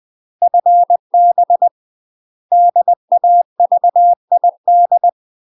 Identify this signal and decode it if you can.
Morse code, heard as a keyed tone, about 20 wpm, 700 Hz. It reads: FB DAVID
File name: UK MASTER QSOs TRAINING / 20WPM / UK MASTER QSOs 0010